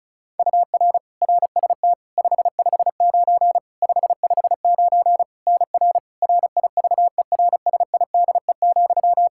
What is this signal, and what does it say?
Morse code, heard as a keyed tone, about 34 wpm, 700 Hz. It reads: UR RST 559 559 NR RIVERSIDE,
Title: Practice QSO - Exchange 1 - Beginning - 35wpm